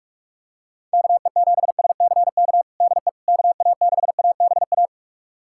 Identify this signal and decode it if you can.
Morse code, heard as a keyed tone, about 46 wpm, 700 Hz. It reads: KE7SCK DE KA6ABA
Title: USA Callsign DE Callsign - 6 Characters - 45wpm